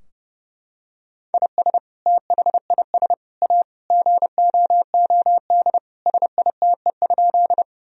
Morse code, heard as a keyed tone, about 32 wpm, 700 Hz. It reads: IS THIS A GOOD SITE?
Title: Sentences from Top 1000 Words - Review Part 4 - 30wpm